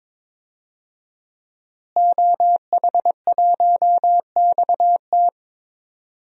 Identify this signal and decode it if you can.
Morse code, heard as a keyed tone, about 21 wpm, 700 Hz. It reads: OH1XT
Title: DXCC-100 Call Signs - 5 Characters - Repeated 3x - 22wpm